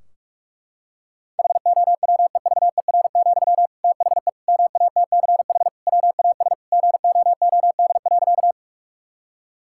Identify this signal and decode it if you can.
Morse code, heard as a keyed tone, about 46 wpm, 700 Hz. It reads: HOWEVER, THE MATCH WAS GOOD.
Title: Sentences from Top 1600 Words 45wpm